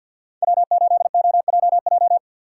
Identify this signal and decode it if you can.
Morse code, heard as a keyed tone, about 60 wpm, 700 Hz. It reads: W8OJJ
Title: Speed Racing - USA 1x3 Call Signs 50wpm